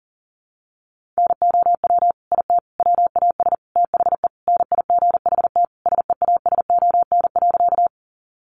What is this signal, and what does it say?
Morse code, heard as a keyed tone, about 40 wpm, 700 Hz. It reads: NOW IT WAS THE NIGHT SEASON.
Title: Sentences from Top 1400 Words 40wpm